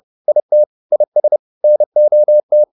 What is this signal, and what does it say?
Morse code, heard as a keyed tone, about 30 wpm, 600 Hz. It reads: IT IS NOT